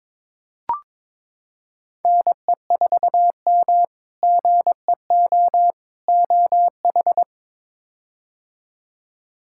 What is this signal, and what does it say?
Morse code, heard as a keyed tone, about 22 wpm, 700 Hz. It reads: NE4M GEO OH
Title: K1USN SST QSO Practice - 22wpm